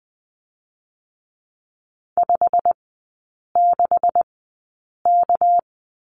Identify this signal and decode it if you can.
Morse code, heard as a keyed tone, about 20 wpm, spaced as for 8 wpm, 700 Hz. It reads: 56K
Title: 218 Lesson 35e - Number 0 - Three Characters 20wpm